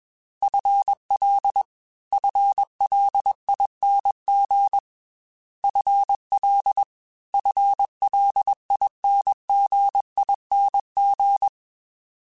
Morse code, heard as a keyed tone, about 22 wpm, 800 Hz. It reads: FL FLING FL FLINGING
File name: FL / FL mp3s / FL Prefixes 1x1 @20